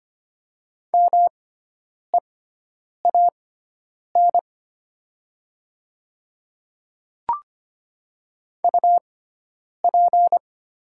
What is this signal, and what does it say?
Morse code, heard as a keyed tone, about 25 wpm, spaced as for 9 wpm, 700 Hz. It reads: MEAN UP
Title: 6x Character Spacing - Top 200 Words - 25wpm